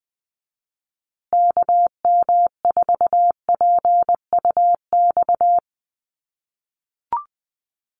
Morse code, heard as a keyed tone, about 20 wpm, 700 Hz. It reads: KM4PUX
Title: USA Call Signs Repeated 3x 20wpm